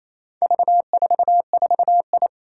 Morse code, heard as a keyed tone, about 27 wpm, 700 Hz. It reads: V44I
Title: Four Challenging Letter-Number Combinations 28wpm